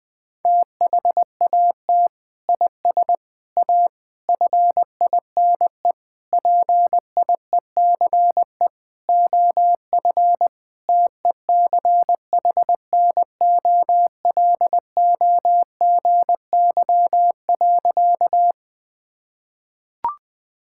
Morse code, heard as a keyed tone, about 20 wpm, 700 Hz. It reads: THAT IS A FINE PIECE OF TECHNOLOGY.
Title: Sentences from Top 1000 Words - Review Part 5 - 20wpm